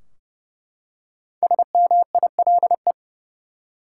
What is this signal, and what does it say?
Morse code, heard as a keyed tone, about 30 wpm, 700 Hz. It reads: SMILE
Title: Five Letter Words - 30wpm